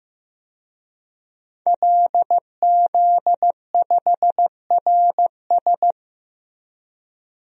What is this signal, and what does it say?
Morse code, heard as a keyed tone, about 15 wpm, 700 Hz. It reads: LZ5RS
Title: DXCC-100 Call Signs - 5 Characters - Repeated 2x - 15wpm